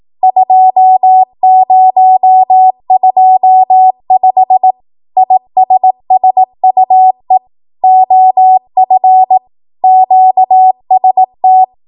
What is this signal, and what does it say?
Morse code, heard as a keyed tone, about 18 wpm, 750 Hz. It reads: 2025 ISSUE OF QST
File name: Archive / 18 WPM / 18WPM